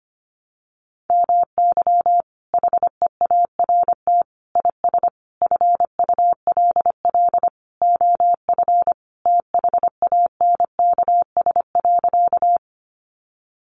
Morse code, heard as a keyed tone, about 25 wpm, 700 Hz. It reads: MY HEART IS FULL OF THANKS.
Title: Sentences from Top 1200 Words 25wpm